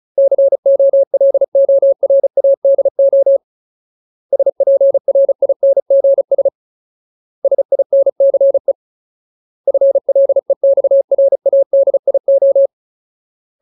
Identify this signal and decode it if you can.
Morse code, heard as a keyed tone, about 34 wpm, 550 Hz. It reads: COLORADO SPRINGS SINCE FLEXRADIO
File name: Ultimate Word List / 35WPM Ultimate